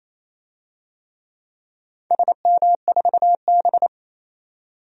Morse code, heard as a keyed tone, about 28 wpm, 700 Hz. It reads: SM4B
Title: DXCC-100 Call Signs - 4 Characters - 28wpm